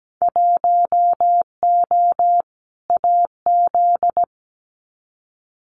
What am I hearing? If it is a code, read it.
Morse code, heard as a keyed tone, about 17 wpm, 700 Hz. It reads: 1O AZ
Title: Winter Field Day Exchanges 17wpm